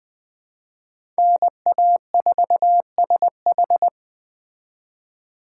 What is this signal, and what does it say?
Morse code, heard as a keyed tone, about 20 wpm, 700 Hz. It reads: NA4SH
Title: Speed Racing - USA 2x2 Call Signs 17wpm